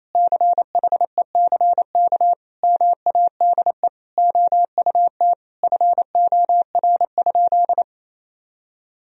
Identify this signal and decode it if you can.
Morse code, heard as a keyed tone, about 28 wpm, 700 Hz. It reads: CHECK MADE OUT FOR?
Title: Sentences from Top 1000 Words - Review Part 3 - 28wpm